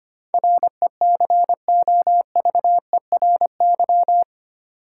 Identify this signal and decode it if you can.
Morse code, heard as a keyed tone, about 25 wpm, 690 Hz. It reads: RECOVERY
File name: 8 Letter Words - First 3 Letters Given - 25wpm